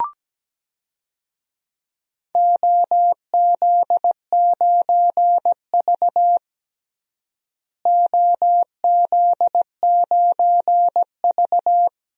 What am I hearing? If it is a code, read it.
Morse code, heard as a keyed tone, about 17 wpm, 700 Hz. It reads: OZ9V OZ9V
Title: DXCC-100 Call Signs - 4 Characters - Repeated 2x - 17wpm